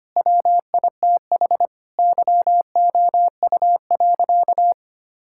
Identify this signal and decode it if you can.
Morse code, heard as a keyed tone, about 25 wpm, 700 Hz. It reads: WITH YOU.
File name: practice / 25wpm / Sentences with 4-Word Phrases - 25wpm